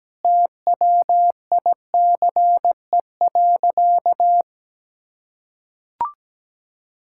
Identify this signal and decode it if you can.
Morse code, heard as a keyed tone, about 17 wpm, 700 Hz. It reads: TWICE.
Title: Sentences from Top 1600 Words 17wpm